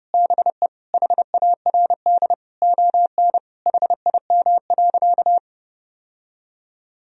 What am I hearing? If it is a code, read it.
Morse code, heard as a keyed tone, about 30 wpm, 700 Hz. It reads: BE HARD ON HIM.
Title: Sentences from Top 1000 Words - Review Part 1 - 30wpm